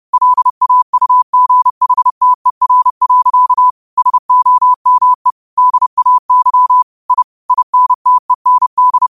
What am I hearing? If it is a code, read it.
Morse code, heard as a keyed tone, about 30 wpm, 1000 Hz. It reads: LAUGHTER. SOME DAY I INTEND
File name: qotdfiles / 30wpm